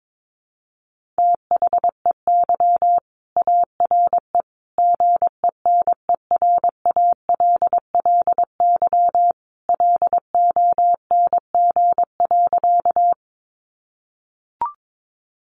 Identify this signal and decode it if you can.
Morse code, heard as a keyed tone, about 22 wpm, 700 Hz. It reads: THEY ARE GENERALLY LONG.
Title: Sentences from Top 600 Words 22wpm